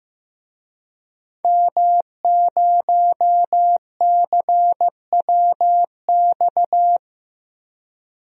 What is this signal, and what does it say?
Morse code, heard as a keyed tone, about 16 wpm, 700 Hz. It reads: M0CWX
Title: DXCC-100 Call Signs - 5 Characters - Repeated 3x - 15wpm